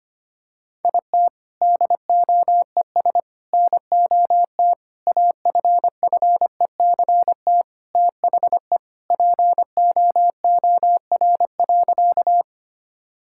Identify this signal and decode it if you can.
Morse code, heard as a keyed tone, about 25 wpm, 700 Hz. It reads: IT DOES NOT AFFECT THE POOR.
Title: Sentences from Top 1000 Words - Review Part 6 - 25wpm